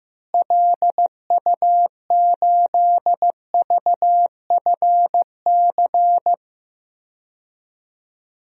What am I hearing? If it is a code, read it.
Morse code, heard as a keyed tone, about 15 wpm, 700 Hz. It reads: LU8VFC
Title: DXCC-100 Call Signs - 6 Characters - Repeated 3x - 15wpm